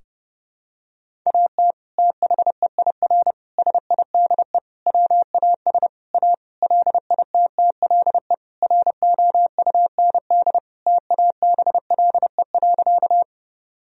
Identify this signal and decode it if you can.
Morse code, heard as a keyed tone, about 30 wpm, 700 Hz. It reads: AT THEIR SIDE WAS A LITTLE ROUND TABLE.
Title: Sentences from Top 1000 Words - Review Part 2 - 30wpm